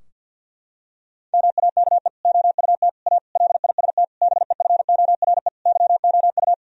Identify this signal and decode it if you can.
Morse code, heard as a keyed tone, about 53 wpm, 700 Hz. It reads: MAKE OUT A LIST BEFORE YOU
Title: Sentences from Top 1000 Words - Review Part 2 - 50wpm